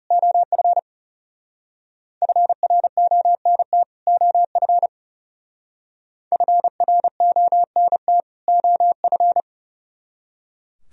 Morse code, heard as a keyed tone, about 40 wpm, 700 Hz. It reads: OF FRONT OF FRONT OF